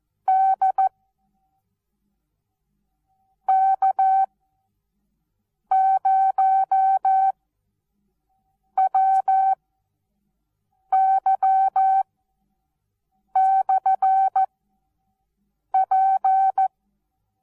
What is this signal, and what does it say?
Morse code, heard as a keyed tone, about 14 wpm, spaced as for 5 wpm, 770 Hz. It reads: DK0WY/P
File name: Onlinelehrgang / morsen / DJ4UF Morsen19